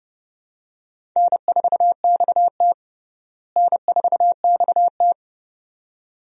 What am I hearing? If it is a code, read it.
Morse code, heard as a keyed tone, about 32 wpm, 700 Hz. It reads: N4XT N4XT
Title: USA Call Signs - 4 Characters - Repeated 2x - 30wpm